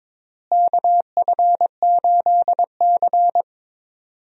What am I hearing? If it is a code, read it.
Morse code, heard as a keyed tone, about 22 wpm, 700 Hz. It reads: KF8C
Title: USA Call Signs - 4 Characters - Repeated 2x - 22wpm